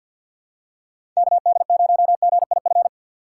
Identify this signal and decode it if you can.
Morse code, heard as a keyed tone, about 52 wpm, 700 Hz. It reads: KD0GIF
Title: USA Call Signs - 6 Characters - 3x Character Spacing - 50wpm